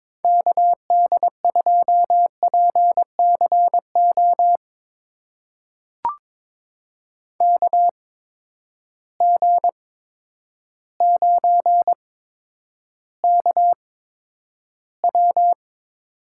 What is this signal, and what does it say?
Morse code, heard as a keyed tone, about 22 wpm, spaced as for 10 wpm, 700 Hz. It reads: KD2PCO KG9KW
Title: USA Call Signs - 8x Character Spacing - 22wpm